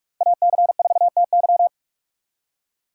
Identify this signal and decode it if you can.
Morse code, heard as a keyed tone, about 46 wpm, 700 Hz. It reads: AC4TY
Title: USA Call Signs - 5 Characters - 3x Character Spacing - 45wpm